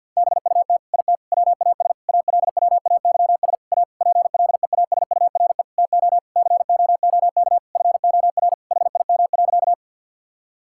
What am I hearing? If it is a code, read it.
Morse code, heard as a keyed tone, about 50 wpm, 700 Hz. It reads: BUT IT WAS ALWAYS A PLEASURE TO COOK FOR HIM.
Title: Sentences from Top 1600 Words 50wpm